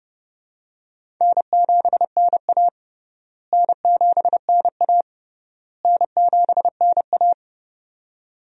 Morse code, heard as a keyed tone, about 30 wpm, 700 Hz. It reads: N7NA N7NA N7NA
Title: USA Call Signs - 4 Characters - Repeated 3x - 30wpm